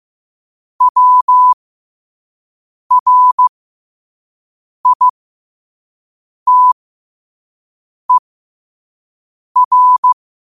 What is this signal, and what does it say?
Morse code, heard as a keyed tone, about 15 wpm, spaced as for 5 wpm, 1000 Hz. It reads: WRITER
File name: qotdfiles / 5wpm